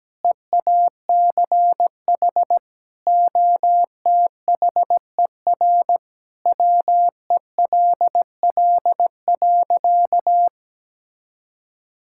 Morse code, heard as a keyed tone, about 17 wpm, 700 Hz. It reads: EACH OTHER WELL.
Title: Sentences from Top 1000 Words - Review Part 5 - 17wpm